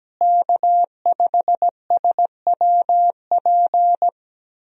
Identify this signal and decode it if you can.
Morse code, heard as a keyed tone, about 17 wpm, 700 Hz. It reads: K5SWP
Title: USA Call Signs - 5 Characters - Repeated 2x - 17wpm